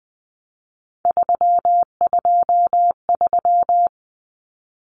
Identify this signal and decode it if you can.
Morse code, heard as a keyed tone, about 20 wpm, 700 Hz. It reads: 323
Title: Common Cut Numbers - 3-Digit 20wpm